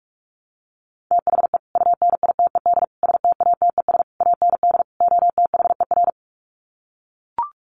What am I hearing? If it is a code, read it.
Morse code, heard as a keyed tone, about 44 wpm, 700 Hz. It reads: THE UNITED STATES AND OTHER
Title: Sets of 5 Words Encore 45wpm